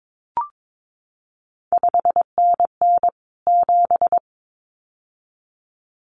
Morse code, heard as a keyed tone, about 22 wpm, 700 Hz. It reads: 5NN 7
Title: CQ WW DX Contest Exchange Encore - 22wpm